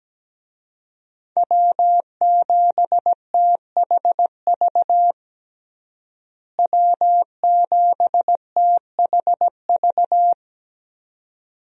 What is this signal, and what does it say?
Morse code, heard as a keyed tone, about 17 wpm, 700 Hz. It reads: W7THV W7THV
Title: USA Call Signs - 5 Characters - Repeated 2x - 17wpm